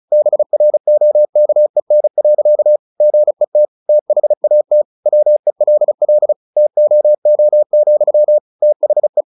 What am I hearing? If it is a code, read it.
Morse code, heard as a keyed tone, about 34 wpm, 600 Hz. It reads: BROKEN. GET THAT WELL TOO, THE
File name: the old man and the sea 2 35wpm